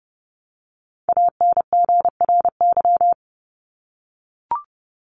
Morse code, heard as a keyed tone, about 30 wpm, 700 Hz. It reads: ANGRY
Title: Top 1751-2000 Words 30wpm